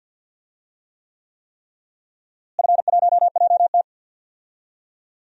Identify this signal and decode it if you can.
Morse code, heard as a keyed tone, about 52 wpm, 700 Hz. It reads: F1JT
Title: DXCC-100 Call Signs - 4 Characters - Repeated 3x - 50wpm